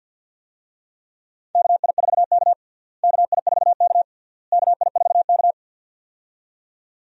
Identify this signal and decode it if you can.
Morse code, heard as a keyed tone, about 52 wpm, 700 Hz. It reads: KI4K KI4K KI4K
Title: USA Call Signs - 4 Characters - Repeated 3x - 50wpm